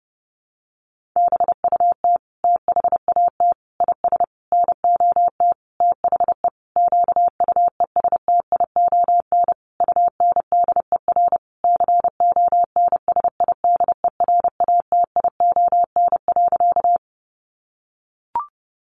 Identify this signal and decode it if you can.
Morse code, heard as a keyed tone, about 30 wpm, 700 Hz. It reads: BUT THAT IS NOT THE QUESTION UNDER CONSIDERATION.
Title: Sentences from Top 1500 Words 30wpm